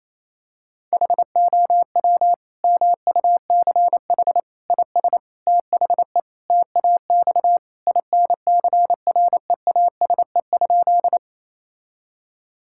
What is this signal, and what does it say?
Morse code, heard as a keyed tone, about 28 wpm, 700 Hz. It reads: HOW MUCH IS THE TAX INCREASE?
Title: Sentences from Top 1000 Words - Review Part 5 - 28wpm